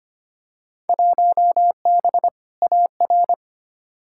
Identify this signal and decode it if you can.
Morse code, heard as a keyed tone, about 25 wpm, 700 Hz. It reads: 1B AR
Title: Field Day Exchanges Encore - 25wpm